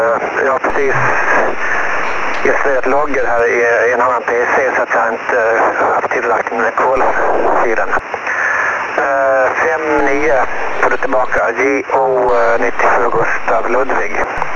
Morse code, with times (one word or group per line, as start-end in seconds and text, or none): none